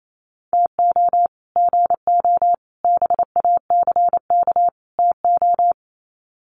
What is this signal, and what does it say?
Morse code, heard as a keyed tone, about 28 wpm, 700 Hz. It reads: TO GO BACK TO
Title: Sets of 4 Words 28wpm